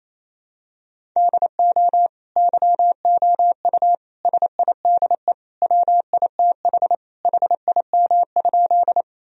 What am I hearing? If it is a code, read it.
Morse code, heard as a keyed tone, about 28 wpm, 700 Hz. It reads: DO YOU SIDE WITH HIM?
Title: Sentences from Top 1000 Words - Review Part 1 - 28wpm